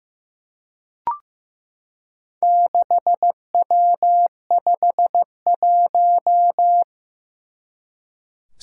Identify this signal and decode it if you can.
Morse code, heard as a keyed tone, about 15 wpm, 700 Hz. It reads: 6W51
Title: Four Challenging Letter-Number Combinations 15wpm